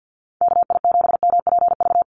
Morse code, heard as a keyed tone, about 49 wpm, 700 Hz. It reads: KI7MPV
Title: USA Call Signs Repeated 3x 50wpm